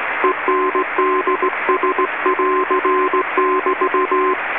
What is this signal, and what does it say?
Morse code, heard as a keyed tone, about 17 wpm, 1040 Hz, with radio noise. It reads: ENDS<AR><BT>
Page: All of these recordings were made by the author onboard the New Zealand inter island rail ferry ARAHURA / ZMBS in November 1993 when I was Radio Officer.